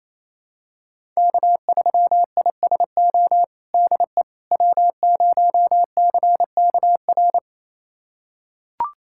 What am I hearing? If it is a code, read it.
Morse code, heard as a keyed tone, about 28 wpm, 700 Hz. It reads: K3ISO DE W0CKR
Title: USA Callsign DE Callsign - 5 Characters - 28wpm